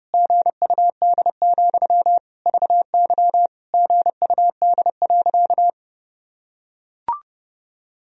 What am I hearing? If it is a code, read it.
Morse code, heard as a keyed tone, about 30 wpm, 700 Hz. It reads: GUD, VY GUD.
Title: Sentences from Top 200 Words and Abbreviations 30wpm